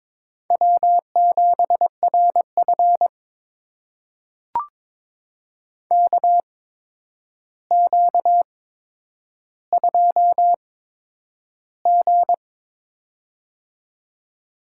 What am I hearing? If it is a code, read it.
Morse code, heard as a keyed tone, about 22 wpm, spaced as for 9 wpm, 700 Hz. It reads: W7RF KQ2G
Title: USA Call Signs - 8x Character Spacing - 22wpm